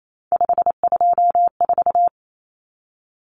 Morse code, heard as a keyed tone, about 28 wpm, 700 Hz. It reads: 524
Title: Common Cut Numbers - 3-Digit 28wpm